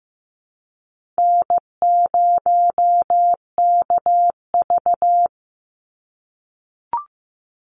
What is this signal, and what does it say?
Morse code, heard as a keyed tone, about 15 wpm, 700 Hz. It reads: N0KV
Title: Easy Call Signs - 15wpm